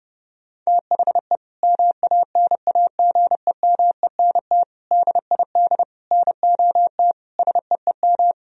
Morse code, heard as a keyed tone, about 32 wpm, 700 Hz. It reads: THE MANAGEMENT DID NOT SEEM
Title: Sentences from Top 1000 Words - Review Part 4 - 30wpm